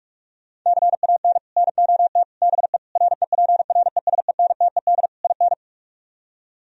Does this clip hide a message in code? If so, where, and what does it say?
Yes: Morse code from 0.7 to 5.5 s, heard as a keyed tone, about 44 wpm, 700 Hz. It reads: CAN NOT BE REPRESENTED IN